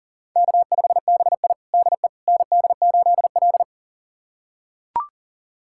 Morse code, heard as a keyed tone, about 37 wpm, 700 Hz. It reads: K5BI DE ND8L
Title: USA Callsign DE Callsign - 4 Characters - 40wpm